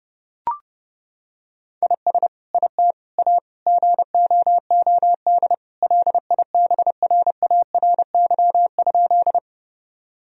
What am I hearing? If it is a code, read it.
Morse code, heard as a keyed tone, about 32 wpm, 700 Hz. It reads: IS IT A GOOD LIBRARY?
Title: Sentences from Top 1000 Words - Review Part 4 - 30wpm